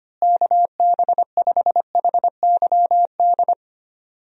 Morse code, heard as a keyed tone, about 25 wpm, 700 Hz. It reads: KB5HYD
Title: USA Call Signs - 6 Characters - Repeated 2x - 25wpm